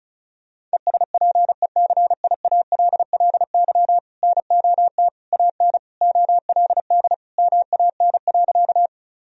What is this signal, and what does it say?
Morse code, heard as a keyed tone, about 36 wpm, 700 Hz. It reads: ESPECIALLY NOT AN OLD MAN.
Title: Sentences from Top 1000 Words - Review Part 4 - 35wpm